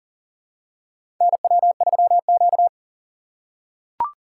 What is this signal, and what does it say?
Morse code, heard as a keyed tone, about 40 wpm, 700 Hz. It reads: NW3Q
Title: USA Call Signs - 4 Characters - 3x Character Spacing - 40wpm